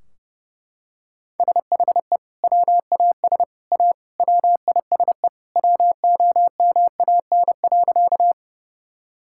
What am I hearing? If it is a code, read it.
Morse code, heard as a keyed tone, about 32 wpm, 700 Hz. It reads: SHE WAS A WISE WOMAN.
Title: Sentences from Top 1600 Words 30wpm